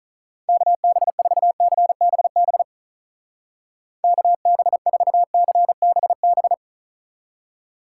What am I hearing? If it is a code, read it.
Morse code, heard as a keyed tone, about 45 wpm, 700 Hz. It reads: KB4CBB KB4CBB